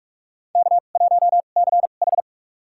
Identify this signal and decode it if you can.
Morse code, heard as a keyed tone, about 47 wpm, spaced as for 33 wpm, 700 Hz. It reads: K1CH
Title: USA Call Signs - 2x Character Spacing - 45wpm